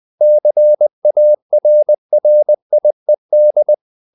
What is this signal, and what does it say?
Morse code, heard as a keyed tone, about 20 wpm, 600 Hz. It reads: CARRIED
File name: the old man and the sea / the old man and the sea part 1 20wpm